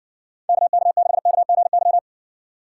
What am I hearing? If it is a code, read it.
Morse code, heard as a keyed tone, about 60 wpm, 700 Hz. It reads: KK6KGY